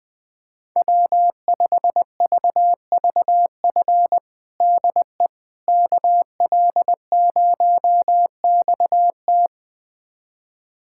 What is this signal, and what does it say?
Morse code, heard as a keyed tone, about 20 wpm, 700 Hz. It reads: W5VVF DE KL0XT
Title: USA Callsign DE Callsign - 5 Characters - 20wpm